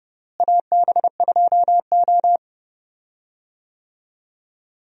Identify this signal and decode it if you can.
Morse code, heard as a keyed tone, about 30 wpm, 700 Hz. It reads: AB2O
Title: USA Call Signs - 4 Characters - Repeated 3x - 30wpm